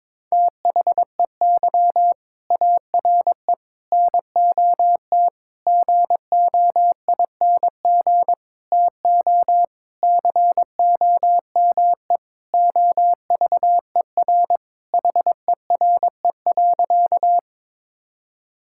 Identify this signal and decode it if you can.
Morse code, heard as a keyed tone, about 22 wpm, 700 Hz. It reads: THEY ARE NOT GOING TO COME OVER HERE.
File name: practice / 22wpm / Sentences with 4-Word Phrases - 22wpm